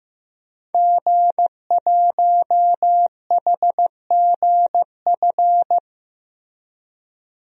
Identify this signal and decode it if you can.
Morse code, heard as a keyed tone, about 16 wpm, 700 Hz. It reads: G1HGF
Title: DXCC-100 Call Signs - 5 Characters - Repeated 2x - 15wpm